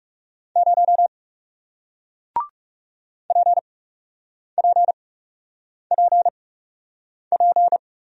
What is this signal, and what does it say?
Morse code, heard as a keyed tone, about 39 wpm, spaced as for 8 wpm, 700 Hz. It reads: 0 PPPP